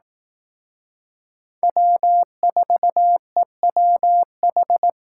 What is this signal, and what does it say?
Morse code, heard as a keyed tone, about 18 wpm, 700 Hz. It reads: W4EWH
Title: Speed Racing - USA 1x3 Call Signs 15wpm